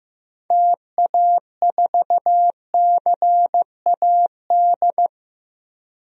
Morse code, heard as a keyed tone, about 15 wpm, 700 Hz. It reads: TA4CAD
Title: DXCC-100 Call Signs - 6 Characters - 15wpm